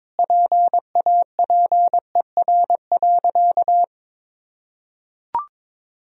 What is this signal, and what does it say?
Morse code, heard as a keyed tone, about 22 wpm, 700 Hz. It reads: PAPER.
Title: Sentences from Top 1000 Words - Review Part 3 - 22wpm